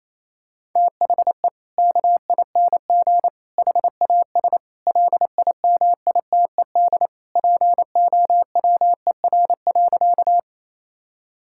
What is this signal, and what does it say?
Morse code, heard as a keyed tone, about 28 wpm, 700 Hz. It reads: THE KING HAS LIMITED POWER.
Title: Sentences from Top 1000 Words - Review Part 6 - 28wpm